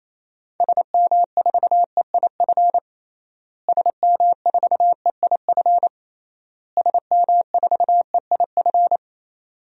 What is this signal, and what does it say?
Morse code, heard as a keyed tone, about 28 wpm, 700 Hz. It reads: SM4EIF SM4EIF SM4EIF
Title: DXCC-100 Call Signs - 6 Characters - Repeated 3x - 28wpm